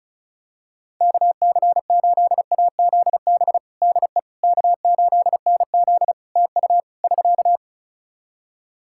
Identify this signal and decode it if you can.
Morse code, heard as a keyed tone, about 36 wpm, 700 Hz. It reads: KC8AZB DE K8NZ TU <SK>
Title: Practice QSO - Exchange 4 - Ending - 35wpm